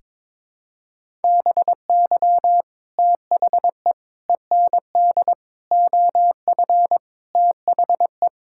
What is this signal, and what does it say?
Morse code, heard as a keyed tone, about 21 wpm, 700 Hz. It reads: BY THE END OF THE
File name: practice / 22wpm / Sentences with 4-Word Phrases - 22wpm